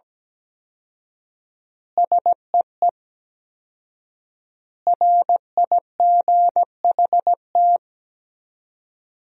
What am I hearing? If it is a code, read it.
Morse code, heard as a keyed tone, about 17 wpm, 700 Hz. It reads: SEE RIGHT
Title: 4x Word-Spacing - Sentences from Top 200 Words - 17wpm